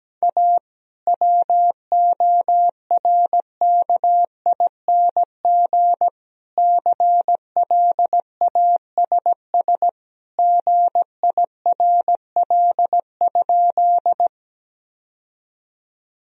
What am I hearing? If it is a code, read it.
Morse code, heard as a keyed tone, about 17 wpm, 700 Hz. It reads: A WORKING CLASS GIRL?
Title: Sentences from Top 1000 Words - Review Part 4 - 17wpm